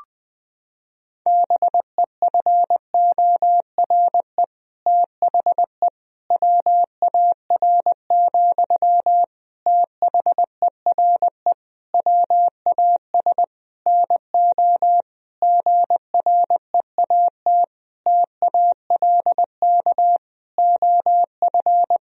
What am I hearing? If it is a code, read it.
Morse code, heard as a keyed tone, about 21 wpm, 700 Hz. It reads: BEFORE THE WAR, THERE WAS NO GREAT TALK OF
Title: Sentences from Top 1000 Words - Review Part 2 - 20wpm